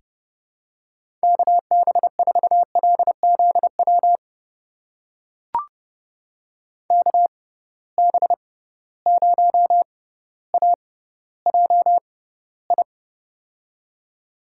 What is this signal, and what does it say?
Morse code, heard as a keyed tone, about 31 wpm, spaced as for 16 wpm, 700 Hz. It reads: KB4LZW KB0AJI
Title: USA Call Signs - 6 Characters - 6x Character Spacing - 30wpm